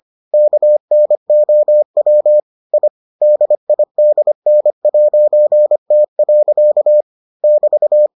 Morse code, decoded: KNOW I DIDN'T. <BT>

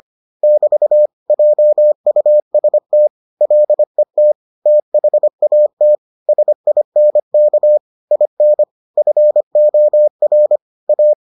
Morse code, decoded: <BT> JUST LET THAT SINK IN FOR A

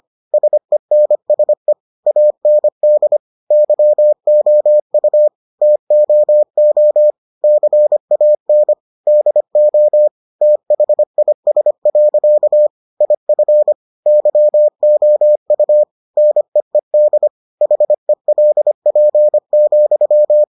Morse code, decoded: SENSE AND YOU TOO CAN DO THIS. IF YOU NEED HELP,

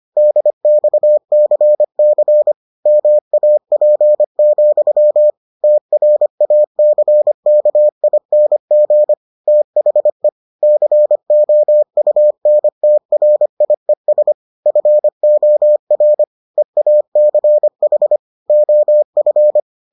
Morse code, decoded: DXCC MAP, TRACKING THE COUNTRIES FOR EACH OF

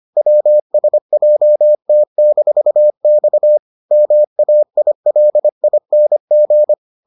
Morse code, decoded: WSJT-X MAILING

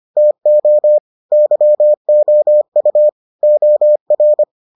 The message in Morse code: TO YOU OR

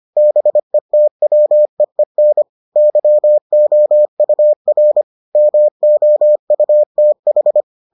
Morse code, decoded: BETWEEN YOUR MOUTH

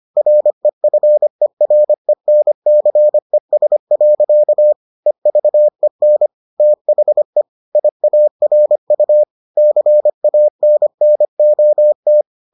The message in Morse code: REFERENCES. EVEN THE IARU CANNOT

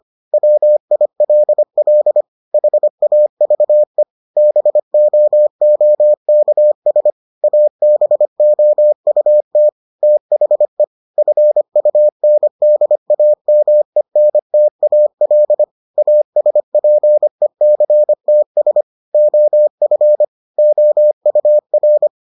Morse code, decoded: WILL HAVE BOOKS ABOUT THE FUNDAMENTAL ASPECTS OF OUR

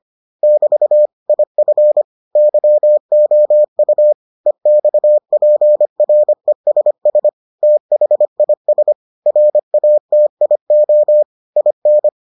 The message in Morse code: <BT> IF YOU EXPRESS THIS RATIO IN

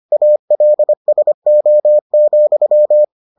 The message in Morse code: ALSO,

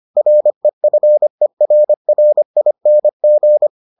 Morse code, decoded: REFERRING

